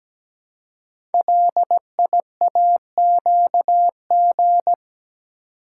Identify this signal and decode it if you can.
Morse code, heard as a keyed tone, about 17 wpm, 700 Hz. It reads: LIAQG
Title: Code Groups - 5 Letters - 17wpm